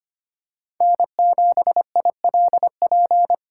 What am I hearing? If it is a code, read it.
Morse code, heard as a keyed tone, about 25 wpm, 700 Hz. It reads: N7ILP
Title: USA Callsign DE Callsign - 5 Characters - 25wpm